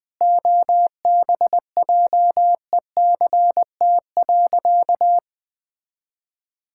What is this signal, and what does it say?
Morse code, heard as a keyed tone, about 20 wpm, 700 Hz. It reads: OBJECT.
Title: Sentences from Top 1000 Words - Review Part 5 - 20wpm